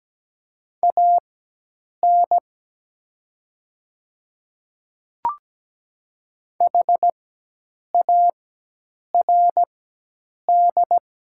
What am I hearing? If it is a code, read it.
Morse code, heard as a keyed tone, about 17 wpm, spaced as for 8 wpm, 700 Hz. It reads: AN HARD
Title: 4x Character Spacing - Top 300 Words - 17wpm